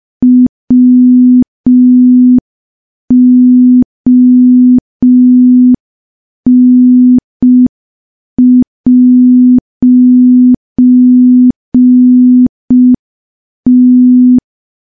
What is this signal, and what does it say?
Morse code, heard as a keyed tone, about 5 wpm, 250 Hz. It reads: WON'T
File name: morse / morse-training-05WPM